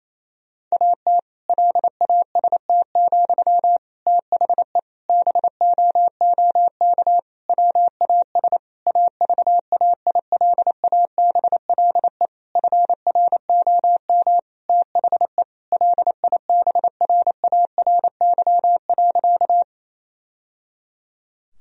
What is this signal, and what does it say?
Morse code, heard as a keyed tone, about 28 wpm, 700 Hz. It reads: AT LAST, THE BOOK WAS AVAILABLE FROM THE LIBRARY.
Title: Sentences from Top 1000 Words - Review Part 3 - 28wpm